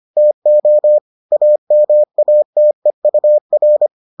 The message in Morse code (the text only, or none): TO AMATEUR